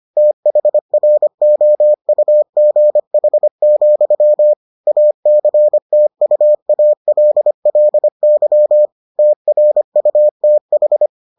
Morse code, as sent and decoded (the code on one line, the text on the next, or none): - .... .-. --- ..- --. .... --..-- / .- -.-. - ..- .- .-.. .-.. -.-- / - .-. ..- - ....
THROUGH, ACTUALLY TRUTH